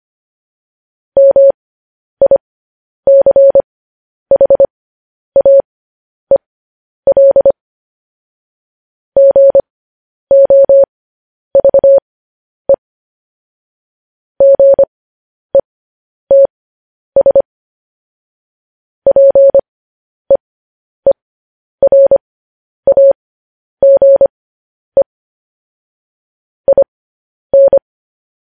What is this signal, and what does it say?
Morse code, heard as a keyed tone, about 25 wpm, spaced as for 10 wpm, 550 Hz. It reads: MICHAEL GOVE GETS PEERAGE IN
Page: news headlines in morse code at 10 wpm